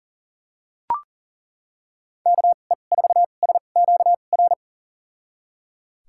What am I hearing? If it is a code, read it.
Morse code, heard as a keyed tone, about 38 wpm, spaced as for 28 wpm, 700 Hz. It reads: KE4SQR
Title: USA Call Signs - 6 Characters - 2x Character Spacing - 40wpm